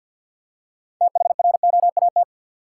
Morse code, heard as a keyed tone, about 51 wpm, 700 Hz. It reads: THROAT
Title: Six Letter Words - 50wpm